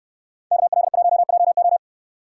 Morse code, heard as a keyed tone, about 69 wpm, 700 Hz. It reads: KL1JO